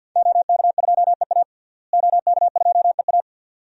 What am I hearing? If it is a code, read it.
Morse code, heard as a keyed tone, about 46 wpm, 700 Hz. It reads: OK2EA OK2EA
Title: DXCC-100 Call Signs - 5 Characters - Repeated 2x - 50wpm